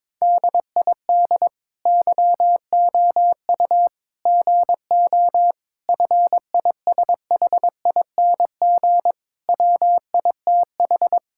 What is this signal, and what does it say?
Morse code, heard as a keyed tone, about 22 wpm, 700 Hz. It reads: DID YOU GO FISHING WITH
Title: Sentences from Top 1000 Words - Review Part 5 - 22wpm